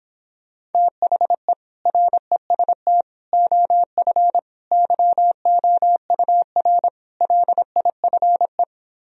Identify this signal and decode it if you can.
Morse code, heard as a keyed tone, about 31 wpm, 700 Hz. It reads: THE REST OF YOUR LIFE